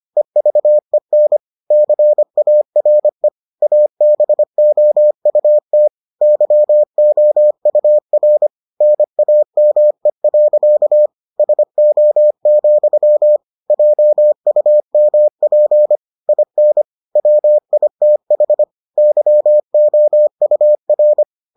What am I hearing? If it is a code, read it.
Morse code, heard as a keyed tone, about 25 wpm, 600 Hz. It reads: EVEN CARE ABOUT YOUR NAME. SO, JUMP IN WITH YOUR